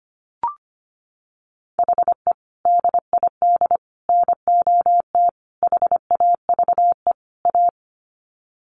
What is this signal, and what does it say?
Morse code, heard as a keyed tone, about 25 wpm, 700 Hz. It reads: HE DID NOT HAVE A
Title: Sets of 5 Words Encore 25wpm